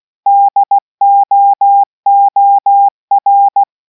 Morse code, heard as a keyed tone, about 16 wpm, 800 Hz. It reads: DOOR